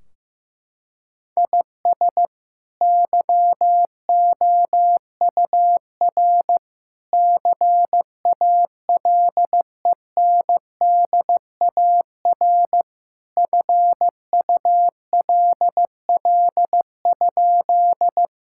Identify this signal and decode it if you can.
Morse code, heard as a keyed tone, about 15 wpm, 700 Hz. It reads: IS YOUR CALENDAR FULL?
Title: Sentences from Top 1600 Words 15wpm